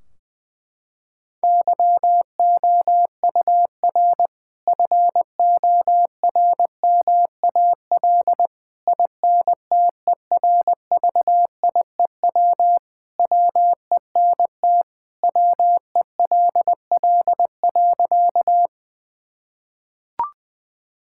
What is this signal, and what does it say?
Morse code, heard as a keyed tone, about 21 wpm, 700 Hz. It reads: YOUR FORMAL INTERVIEW WENT WELL.
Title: Sentences from Top 1600 Words 20wpm